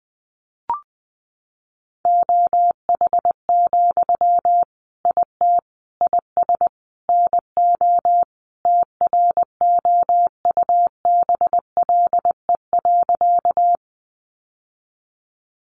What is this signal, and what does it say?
Morse code, heard as a keyed tone, about 20 wpm, 700 Hz. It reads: OH, IT IS NO TROUBLE.
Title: Sentences from Top 1200 Words 20wpm